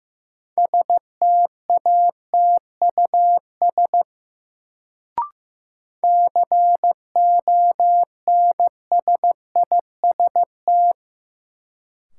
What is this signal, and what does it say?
Morse code, heard as a keyed tone, about 15 wpm, 700 Hz. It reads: STATUS CONSIST
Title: Sentences from Top 1400 Words - New Word Drill - 15wpm